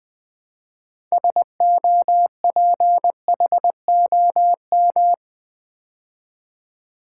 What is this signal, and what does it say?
Morse code, heard as a keyed tone, about 20 wpm, 700 Hz. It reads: SOPHOM
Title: Build A Word - 3 Letters At A Time - 9-Letter Words - 20wpm